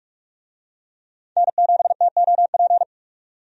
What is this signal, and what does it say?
Morse code, heard as a keyed tone, about 46 wpm, 700 Hz. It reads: N7TOP
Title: USA Call Signs - 5 Characters - 45wpm